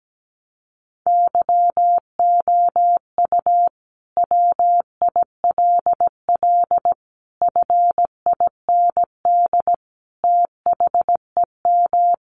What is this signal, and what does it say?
Morse code, heard as a keyed tone, about 17 wpm, 700 Hz. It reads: YOU WILL FIND THEM
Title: Sentences from Top 700 Words 17wpm